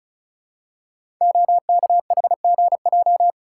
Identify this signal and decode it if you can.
Morse code, heard as a keyed tone, about 36 wpm, 700 Hz. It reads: OKHGJ
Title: Code Groups - 5 Letter:Numbers - 35wpm